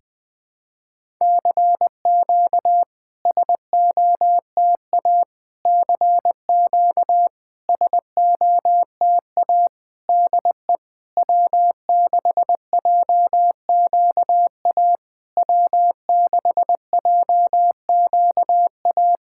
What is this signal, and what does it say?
Morse code, heard as a keyed tone, about 20 wpm, 660 Hz. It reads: CQ SOTA CQ SOTA DE W6JQA W6JQA
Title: SOTA QSO Practice 20wpm